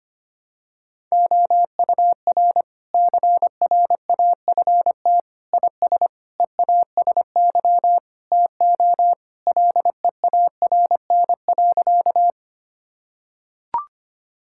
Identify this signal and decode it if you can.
Morse code, heard as a keyed tone, about 25 wpm, 700 Hz. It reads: OUR CRAFT IS EASY TO LEARN.
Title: Sentences from Top 1000 Words - Review Part 6 - 25wpm